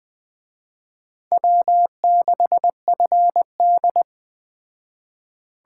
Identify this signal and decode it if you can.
Morse code, heard as a keyed tone, about 21 wpm, 700 Hz. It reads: W6FD